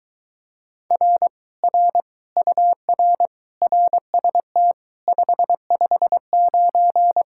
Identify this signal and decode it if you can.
Morse code, heard as a keyed tone, about 22 wpm, 700 Hz. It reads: R R UR RST 559